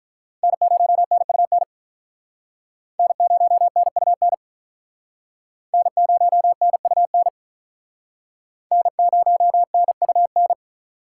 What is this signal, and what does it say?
Morse code, heard as a keyed tone, about 48 wpm, 700 Hz. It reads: N0NUN N0NUN N0NUN N0NUN